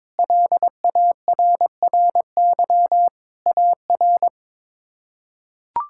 Morse code, heard as a keyed tone, about 22 wpm, 700 Hz. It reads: LARRY AR
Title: K1USN SST QSO Practice - 22wpm